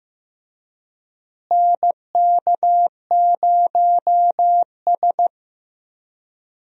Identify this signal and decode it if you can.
Morse code, heard as a keyed tone, about 16 wpm, 700 Hz. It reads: NK0S
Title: USA Call Signs - 4 Characters - Repeated 3x - 15wpm